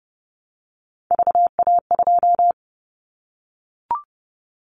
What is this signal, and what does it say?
Morse code, heard as a keyed tone, about 30 wpm, 700 Hz. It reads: VA2
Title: Callsign Prefixes for Top 25 DXCC - 30wpm